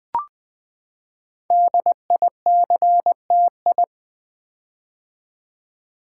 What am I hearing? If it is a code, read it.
Morse code, heard as a keyed tone, about 20 wpm, 700 Hz. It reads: DICTI
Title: Build A Word - 5 Letters At A Time - 10-Letter Words - 20wpm